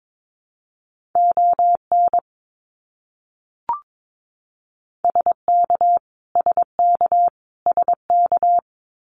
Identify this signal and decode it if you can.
Morse code, heard as a keyed tone, about 22 wpm, 700 Hz. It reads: ON SK SK SK
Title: Canadian Provinces and Territories - Repeated 3x - 22wpm